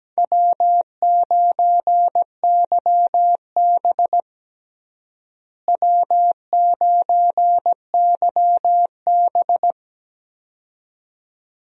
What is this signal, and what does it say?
Morse code, heard as a keyed tone, about 17 wpm, 700 Hz. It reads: W9YB W9YB
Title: USA Call Signs - 4 Characters - Repeated 2x - 17wpm